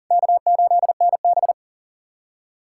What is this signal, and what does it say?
Morse code, heard as a keyed tone, about 40 wpm, 700 Hz. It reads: K8NB
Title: USA Callsign DE Callsign - 4 Characters - 40wpm